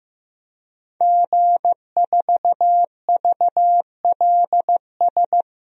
Morse code, heard as a keyed tone, about 15 wpm, 700 Hz. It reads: G4VLS
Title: DXCC-100 Call Signs - 5 Characters - Repeated 2x - 15wpm